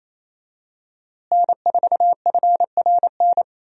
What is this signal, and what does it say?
Morse code, heard as a keyed tone, about 28 wpm, 700 Hz. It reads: N4FRN
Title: USA Call Signs - 5 Characters - Repeated 3x - 28wpm